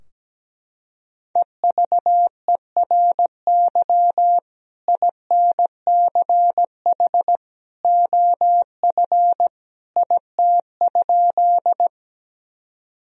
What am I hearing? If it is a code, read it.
Morse code, heard as a keyed tone, about 17 wpm, 700 Hz. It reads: EVERY INCH OF IT?
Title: Sentences from Top 1000 Words - Review Part 3 - 17wpm